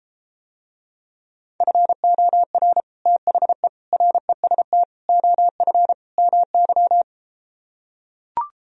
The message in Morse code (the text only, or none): FOR THE REST OF MY